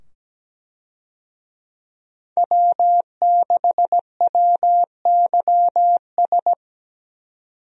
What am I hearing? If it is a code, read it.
Morse code, heard as a keyed tone, about 17 wpm, 700 Hz. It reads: W6WYS
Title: USA Call Signs - 5 Characters - 17wpm